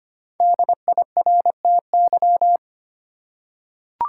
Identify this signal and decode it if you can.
Morse code, heard as a keyed tone, about 25 wpm, 700 Hz. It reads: DIRTY
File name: 5 Letter Words - First 3 Letters Given - 25wpm